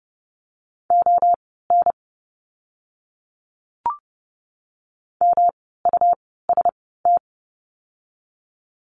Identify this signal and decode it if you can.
Morse code, heard as a keyed tone, about 30 wpm, spaced as for 17 wpm, 700 Hz. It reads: ON MUST
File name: course / 30:10wpm / 115 Lesson 19f - Letter w - Words 30wpm